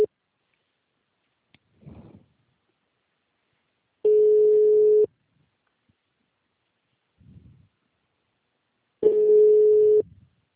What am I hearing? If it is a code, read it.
Ringback tone (Europe, 425 Hz).